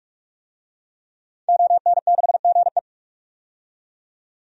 Morse code, heard as a keyed tone, about 46 wpm, 700 Hz. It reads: ON6GE
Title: DXCC-100 Call Signs - 5 Characters - 45wpm